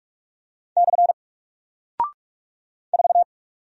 Morse code, heard as a keyed tone, about 44 wpm, 700 Hz. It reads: / 4
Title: Single Character (Letter:Number:Punctuation:Prosign) Alt 45wpm